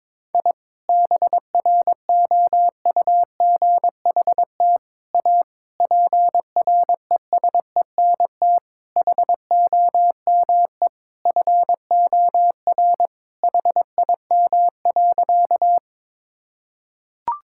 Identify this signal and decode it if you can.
Morse code, heard as a keyed tone, about 21 wpm, 700 Hz. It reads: I BROUGHT A PRESENT HOME FOR HIM.
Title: Sentences from Top 1000 Words - Review Part 5 - 22wpm